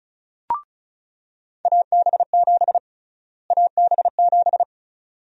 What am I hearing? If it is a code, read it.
Morse code, heard as a keyed tone, about 36 wpm, 700 Hz. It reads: AB7 AB7
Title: USA Call Sign Prefixes - Repeated 2x - 35wpm